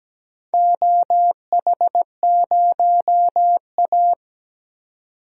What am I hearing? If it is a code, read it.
Morse code, heard as a keyed tone, about 17 wpm, 700 Hz. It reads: OH0A
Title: DXCC-100 Call Signs - 4 Characters - 17wpm